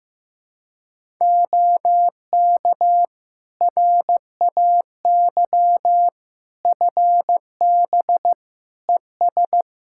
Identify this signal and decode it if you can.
Morse code, heard as a keyed tone, about 16 wpm, 700 Hz. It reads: OK RAY FB ES
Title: Practice QSO - Exchange 2 - Info - 15wpm